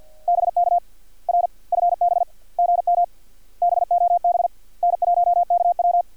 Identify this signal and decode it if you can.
Morse code, heard as a keyed tone, about 50 wpm, 700 Hz. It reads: BK R FB GM BOB N1KW
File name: archived / CRSnet